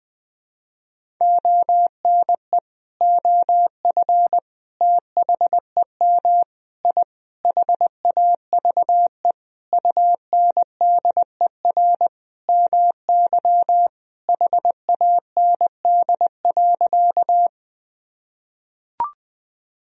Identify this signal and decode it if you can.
Morse code, heard as a keyed tone, about 20 wpm, 700 Hz. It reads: ONE OF THEM I HAVE UNDER MY HAND.
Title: Sentences from Top 1000 Words - Review Part 1 - 20wpm